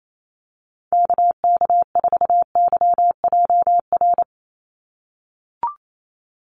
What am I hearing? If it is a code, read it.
Morse code, heard as a keyed tone, about 28 wpm, 700 Hz. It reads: KK4YJR
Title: USA Call Signs 28wpm